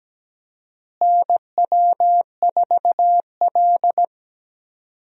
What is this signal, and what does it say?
Morse code, heard as a keyed tone, about 17 wpm, 700 Hz. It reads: NW4L
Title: USA Call Signs - 4 Characters - Repeated 3x - 17wpm